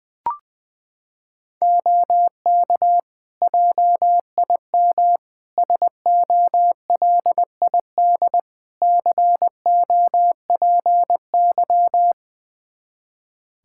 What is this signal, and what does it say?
Morse code, heard as a keyed tone, about 21 wpm, 700 Hz. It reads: OK JIM SOLID COPY
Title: Practice QSO - Exchange 3 - More Info - 20wpm